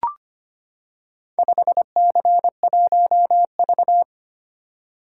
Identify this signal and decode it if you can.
Morse code, heard as a keyed tone, about 25 wpm, 700 Hz. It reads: 5C1V
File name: practice / 25wpm / Four Letter-Number Combinations - 25wpm